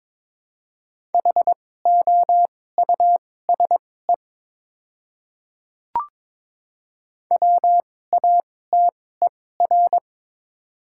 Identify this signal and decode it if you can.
Morse code, heard as a keyed tone, about 22 wpm, spaced as for 16 wpm, 700 Hz. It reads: HOUSE WATER
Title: 2x Character Spacing - Top 200 Words - 22wpm